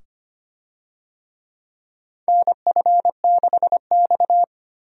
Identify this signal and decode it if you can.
Morse code, heard as a keyed tone, about 25 wpm, 700 Hz. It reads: NF6X
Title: USA Call Signs - 4 Characters - 25wpm